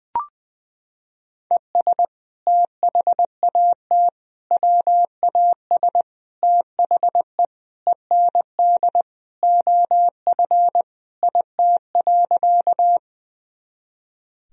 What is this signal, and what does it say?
Morse code, heard as a keyed tone, about 20 wpm, 700 Hz. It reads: ES THAT WAS THE END OF IT.
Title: Sentences from Top 200 Words and Abbreviations 20wpm